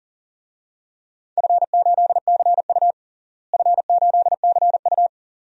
Morse code, heard as a keyed tone, about 40 wpm, 700 Hz. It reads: F8CU F8CU
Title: DXCC-100 Call Signs - 4 Characters - Repeated 2x - 40wpm